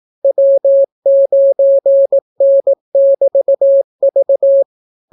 Morse code, heard as a keyed tone, about 18 wpm, 550 Hz. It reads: W9N<BT>V